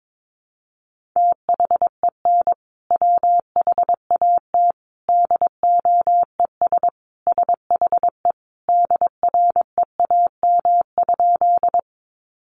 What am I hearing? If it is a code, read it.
Morse code, heard as a keyed tone, about 22 wpm, 700 Hz. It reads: THEN WHAT DOES SHE DREAM?
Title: Sentences from Top 1500 Words 22wpm